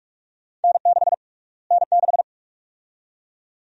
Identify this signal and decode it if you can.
Morse code, heard as a keyed tone, about 43 wpm, 700 Hz. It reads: N6 N6
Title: USA Call Sign Prefixes - Repeated 2x - 45wpm